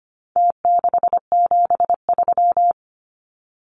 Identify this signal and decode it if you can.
Morse code, heard as a keyed tone, about 25 wpm, 700 Hz. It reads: T673
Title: Numbers - Contest Style - 4-Digit 25wpm